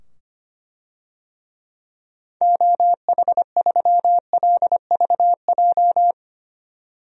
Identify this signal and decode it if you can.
Morse code, heard as a keyed tone, about 25 wpm, 700 Hz. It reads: OH3LVJ
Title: DXCC-100 Call Signs - 6 Characters - 25wpm